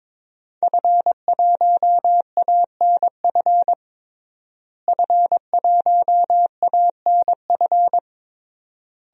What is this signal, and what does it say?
Morse code, heard as a keyed tone, about 22 wpm, 700 Hz. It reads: F1ANF F1ANF
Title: DXCC-100 Call Signs - 5 Characters - Repeated 3x - 22wpm